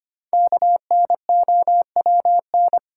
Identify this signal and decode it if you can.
Morse code, heard as a keyed tone, about 25 wpm, 700 Hz. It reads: KNOWN
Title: Sentences from Top 600 Words - New Word Drill - 25wpm